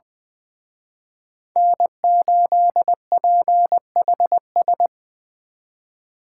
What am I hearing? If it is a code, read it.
Morse code, heard as a keyed tone, about 23 wpm, 700 Hz. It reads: N8PHS
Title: Speed Racing - USA 1x3 Call Signs 20wpm